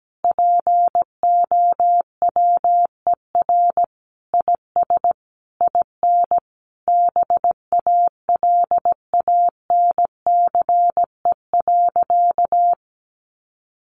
Morse code, heard as a keyed tone, about 17 wpm, 700 Hz. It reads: POWER IS IN BALANCE.
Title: Sentences from Top 900 Words 17wpm